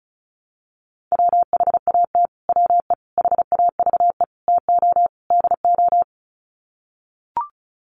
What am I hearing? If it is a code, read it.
Morse code, heard as a keyed tone, about 34 wpm, 700 Hz. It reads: WHAT WE HAVE TO DO
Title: Sets of 5 Words Encore 35wpm